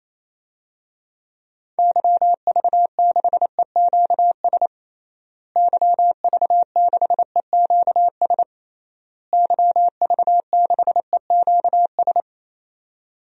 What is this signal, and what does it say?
Morse code, heard as a keyed tone, about 28 wpm, 700 Hz. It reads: YV6EQS YV6EQS YV6EQS
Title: DXCC-100 Call Signs - 6 Characters - Repeated 3x - 28wpm